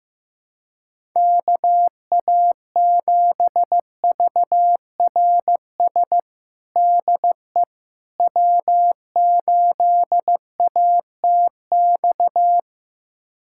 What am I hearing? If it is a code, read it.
Morse code, heard as a keyed tone, about 15 wpm, 700 Hz. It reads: KA7VRS DE W8ATX
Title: USA Callsign DE Callsign 15wpm